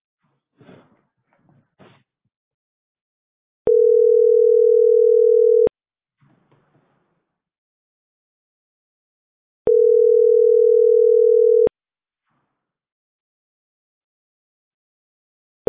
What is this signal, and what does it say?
Ringback tone (North America), 440 + 480 Hz.